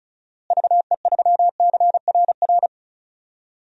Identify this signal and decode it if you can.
Morse code, heard as a keyed tone, about 36 wpm, 700 Hz. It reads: VE3CRR
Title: DXCC-100 Call Signs - 6 Characters - Repeated 3x - 35wpm